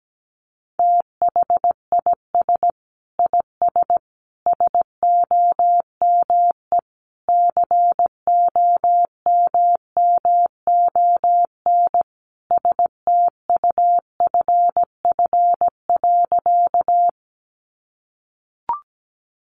Morse code, heard as a keyed tone, about 17 wpm, 700 Hz. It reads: THIS IS SOME COMMON STUFF.
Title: Sentences from Top 1400 Words 17wpm